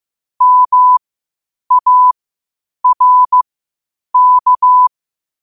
Morse code, heard as a keyed tone, about 15 wpm, spaced as for 8 wpm, 1000 Hz. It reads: MARK
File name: qotdfiles / 8wpm